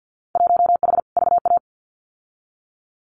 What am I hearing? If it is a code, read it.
Morse code, heard as a keyed tone, about 50 wpm, 700 Hz. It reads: 1H VA
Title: Winter Field Day Exchanges 50wpm